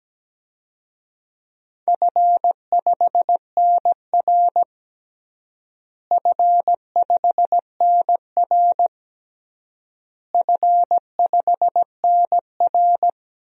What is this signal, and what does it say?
Morse code, heard as a keyed tone, about 17 wpm, 700 Hz. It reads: F5NR F5NR F5NR
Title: DXCC-100 Call Signs - 4 Characters - Repeated 3x - 17wpm